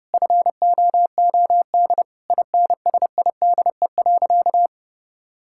Morse code, heard as a keyed tone, about 28 wpm, 700 Hz. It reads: FOOD INSIDE.
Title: Sentences from Top 1000 Words - Review Part 6 - 30wpm